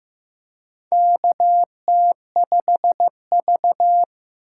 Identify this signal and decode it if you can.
Morse code, heard as a keyed tone, about 15 wpm, 700 Hz. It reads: KT5V
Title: K1USN SST QSO Practice - 15wpm